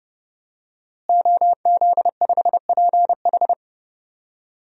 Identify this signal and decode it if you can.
Morse code, heard as a keyed tone, about 30 wpm, 700 Hz. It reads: OZ5PH
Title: DXCC-100 Call Signs - 5 Characters - Repeated 3x - 30wpm